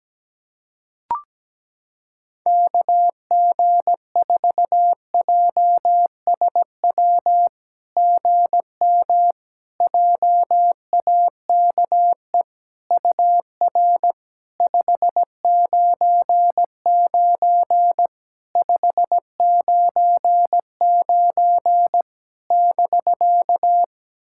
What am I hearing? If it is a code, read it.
Morse code, heard as a keyed tone, about 17 wpm, 700 Hz. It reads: KG4JSW GM JAKE UR 599 599 <BK>
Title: SOTA QSO Practice 17wpm